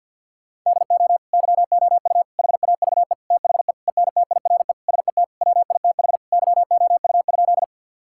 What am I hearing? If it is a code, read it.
Morse code, heard as a keyed tone, about 50 wpm, 700 Hz. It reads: DO YOU HAVE THE ENTIRE SET WITH YOU?
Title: Sentences from Top 1000 Words - Review Part 5 - 50wpm